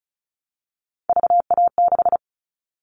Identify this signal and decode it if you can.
Morse code, heard as a keyed tone, about 36 wpm, 700 Hz. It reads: VA6
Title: Callsign Prefixes for Top 100 DXCC - 35wpm